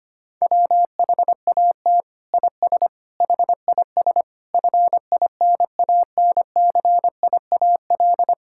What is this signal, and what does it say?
Morse code, heard as a keyed tone, about 25 wpm, 700 Hz. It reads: WHAT IS HIS FINANCIAL
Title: Sentences from Top 1000 Words - Review Part 5 - 25wpm